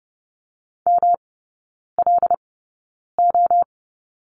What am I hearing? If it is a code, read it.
Morse code, heard as a keyed tone, about 30 wpm, spaced as for 9 wpm, 700 Hz. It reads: MLO
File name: Lesson 18e - Letter m - Three Characters 30wpm